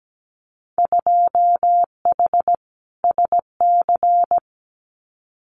Morse code, heard as a keyed tone, about 17 wpm, 700 Hz. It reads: 2H SC
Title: Winter Field Day Exchanges 17wpm